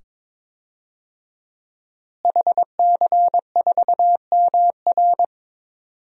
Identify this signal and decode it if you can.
Morse code, heard as a keyed tone, about 22 wpm, 700 Hz. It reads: HC4MR
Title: DXCC-100 Call Signs - 5 Characters - Repeated 2x - 22wpm